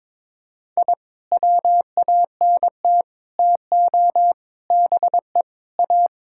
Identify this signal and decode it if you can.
Morse code, heard as a keyed tone, about 22 wpm, 700 Hz. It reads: I WANT TO BE A
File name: practice / 22wpm / Sentences with 3-Word Phrases - 22wpm